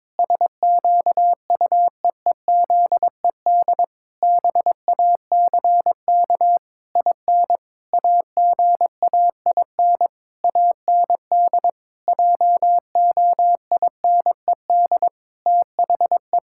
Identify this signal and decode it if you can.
Morse code, heard as a keyed tone, about 22 wpm, 700 Hz. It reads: SQUEEZED BACK IN AGAIN AND JOINED THE
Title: Pangram Challenge v2 - 22wpm